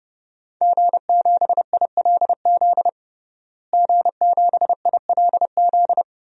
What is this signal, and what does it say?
Morse code, heard as a keyed tone, about 32 wpm, 700 Hz. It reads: G7ILZ G7ILZ
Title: DXCC-100 Call Signs - 5 Characters - Repeated 3x - 30wpm